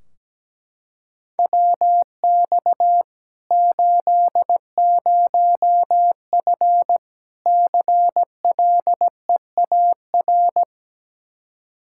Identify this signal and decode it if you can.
Morse code, heard as a keyed tone, about 17 wpm, 700 Hz. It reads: WX 80F CLEAR
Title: Practice QSO - Exchange 2 - Info - 17wpm